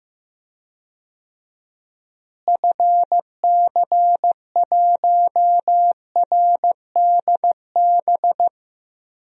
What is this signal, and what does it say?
Morse code, heard as a keyed tone, about 15 wpm, 700 Hz. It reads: FC1RDB
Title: DXCC-100 Call Signs - 6 Characters - Repeated 3x - 15wpm